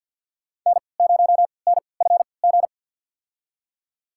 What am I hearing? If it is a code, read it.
Morse code, heard as a keyed tone, about 48 wpm, spaced as for 28 wpm, 700 Hz. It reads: N0NFG
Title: USA Call Signs - 5 Characters - 3x Character Spacing - 50wpm